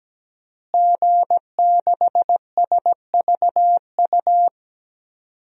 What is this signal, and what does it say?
Morse code, heard as a keyed tone, about 17 wpm, 700 Hz. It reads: G6SVU
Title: DXCC-100 Call Signs - 5 Characters - Repeated 3x - 17wpm